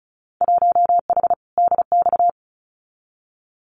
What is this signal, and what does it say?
Morse code, heard as a keyed tone, about 36 wpm, 700 Hz. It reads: 1H DX
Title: Winter Field Day Exchanges 35wpm